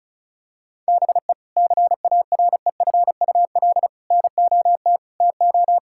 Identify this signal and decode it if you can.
Morse code, heard as a keyed tone, about 34 wpm, 700 Hz. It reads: BE CAREFUL NOT TO
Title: Sentences from Top 1600 Words 35wpm